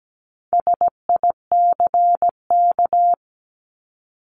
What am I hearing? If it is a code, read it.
Morse code, heard as a keyed tone, about 17 wpm, 700 Hz. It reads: SICK
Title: Top 1501-1750 Words 17wpm